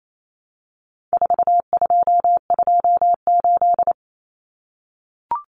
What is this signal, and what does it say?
Morse code, heard as a keyed tone, about 28 wpm, 700 Hz. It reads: 4228
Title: Numbers - 4-Digit 28wpm